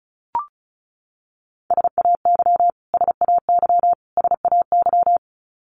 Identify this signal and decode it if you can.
Morse code, heard as a keyed tone, about 34 wpm, 700 Hz. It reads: SAY SAY SAY
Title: Top 500 Words Repeated 35wpm